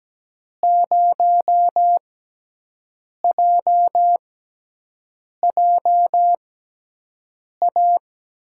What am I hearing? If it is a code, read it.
Morse code, heard as a keyed tone, about 17 wpm, spaced as for 6 wpm, 700 Hz. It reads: 0JJA
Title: USA Call Signs - 6 Characters - 6x Character Spacing - 17wpm